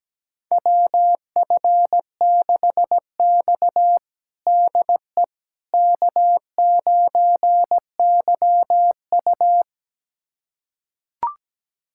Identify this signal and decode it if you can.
Morse code, heard as a keyed tone, about 17 wpm, 700 Hz. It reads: WF6X DE K9YU
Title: USA Callsign DE Callsign - 4 Characters - 17wpm